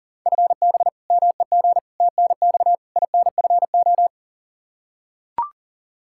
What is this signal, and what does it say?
Morse code, heard as a keyed tone, about 40 wpm, 700 Hz. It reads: FB MEG TNX INFO
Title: Practice QSO - Exchange 3 - More Info - 40wpm